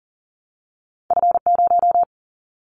Morse code, heard as a keyed tone, about 40 wpm, 700 Hz. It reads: F0
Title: Callsign Prefixes for Top 25 DXCC - Repeated 3x - 40wpm